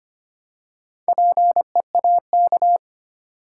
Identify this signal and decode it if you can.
Morse code, heard as a keyed tone, about 25 wpm, 700 Hz. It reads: PEAK
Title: Four Letter Words - 25wpm